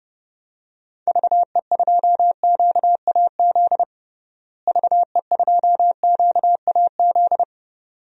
Morse code, heard as a keyed tone, about 30 wpm, 700 Hz. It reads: VE2QAZ VE2QAZ
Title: DXCC-100 Call Signs - 6 Characters - Repeated 2x - 30wpm